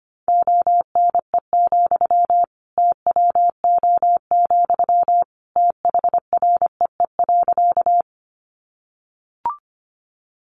Morse code, heard as a keyed tone, about 25 wpm, 700 Hz. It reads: ONE, TWO, THREE.
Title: Sentences from Top 200 Words 25wpm